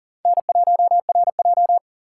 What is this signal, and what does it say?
Morse code, heard as a keyed tone, about 40 wpm, 700 Hz. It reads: N1RJ
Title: USA Call Signs - 4 Characters - 6x Character Spacing - 40wpm